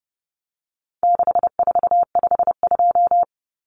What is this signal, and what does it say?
Morse code, heard as a keyed tone, about 30 wpm, 700 Hz. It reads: 6452
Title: Common Cut Numbers - 4-Digit 30wpm